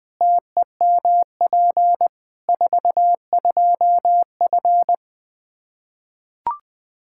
Morse code, heard as a keyed tone, about 20 wpm, 700 Hz. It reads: TEMP 42F
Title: K1USN SST QSO Practice 20wpm